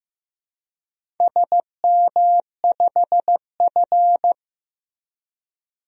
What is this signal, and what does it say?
Morse code, heard as a keyed tone, about 15 wpm, 700 Hz. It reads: SM5F
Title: DXCC-100 Call Signs - 4 Characters - Repeated 3x - 15wpm